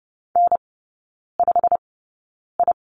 Morse code, heard as a keyed tone, about 30 wpm, spaced as for 9 wpm, 700 Hz. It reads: N5I
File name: Lesson 15e - Number 5 - Three Characters 30wpm